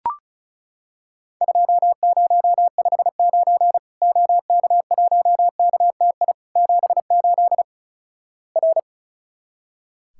Morse code, heard as a keyed tone, about 36 wpm, 700 Hz. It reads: 2059 OK1KTI 78 R
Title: Worked All Europe DX Contest - QTC Exchange - 35wpm